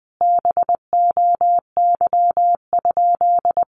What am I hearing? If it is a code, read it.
Morse code, heard as a keyed tone, about 20 wpm, 700 Hz. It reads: BOY?
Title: Sentences from Top 1500 Words 20wpm